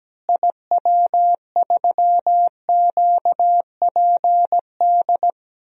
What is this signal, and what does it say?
Morse code, heard as a keyed tone, about 17 wpm, 700 Hz. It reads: IW3QPD
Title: DXCC-100 Call Signs - 6 Characters - Repeated 3x - 17wpm